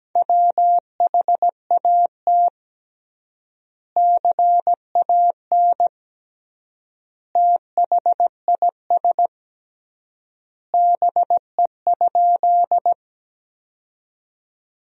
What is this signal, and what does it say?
Morse code, heard as a keyed tone, about 17 wpm, 700 Hz. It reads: WHAT CAN THIS BE?
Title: 3x Word-Spacing - Sentences from Top 100 Words - 17wpm